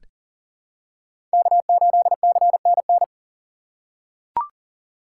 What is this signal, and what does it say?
Morse code, heard as a keyed tone, about 40 wpm, 700 Hz. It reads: K8CNN
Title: USA Call Signs - 4x Character Spacing - 40wpm